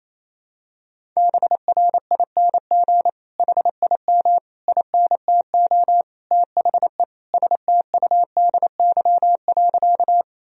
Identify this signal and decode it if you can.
Morse code, heard as a keyed tone, about 27 wpm, 700 Hz. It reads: BRING HIM INTO THE STUDY.
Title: Sentences from Top 1000 Words - Review Part 1 - 28wpm